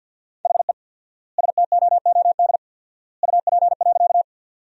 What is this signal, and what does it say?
Morse code, heard as a keyed tone, about 47 wpm, 700 Hz. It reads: HE STOOD UP.
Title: 4x Word-Spacing - Sentences from Top 500 Words - 50wpm